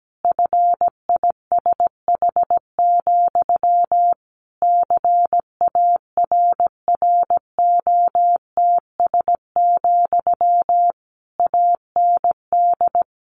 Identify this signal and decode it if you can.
Morse code, heard as a keyed tone, about 17 wpm, 700 Hz. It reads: FISH, CARROTS, AND
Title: Easy Phrases 17wpm